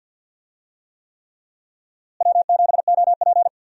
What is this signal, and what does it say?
Morse code, heard as a keyed tone, about 49 wpm, 700 Hz. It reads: W7OP
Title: USA Call Signs - 4 Characters - Repeated 3x - 50wpm